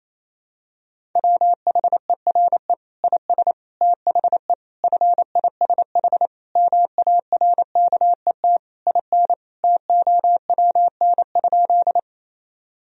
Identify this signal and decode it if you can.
Morse code, heard as a keyed tone, about 28 wpm, 700 Hz. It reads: WHERE IS THE FISH MARKET IN TOWN?
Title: Sentences from Top 1000 Words - Review Part 3 - 28wpm